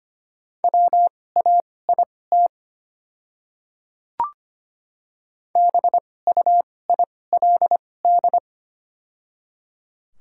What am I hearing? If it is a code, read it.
Morse code, heard as a keyed tone, about 25 wpm, spaced as for 18 wpm, 700 Hz. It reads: WAIT BUILD
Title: 2x Character Spacing - Top 500 Words - 25wpm